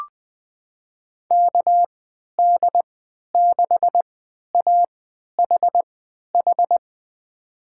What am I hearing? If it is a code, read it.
Morse code, heard as a keyed tone, about 20 wpm, spaced as for 11 wpm, 700 Hz. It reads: KD6AHH
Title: USA Call Signs - 3x Character Spacing - 20wpm